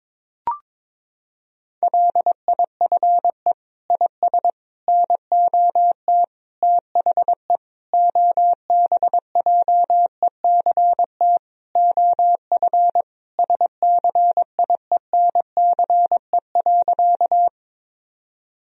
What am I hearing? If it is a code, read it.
Morse code, heard as a keyed tone, about 22 wpm, 700 Hz. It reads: LIFE IS NOT THE OBJECT OF SCIENCE.
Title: Sentences from Top 1000 Words - Review Part 3 - 22wpm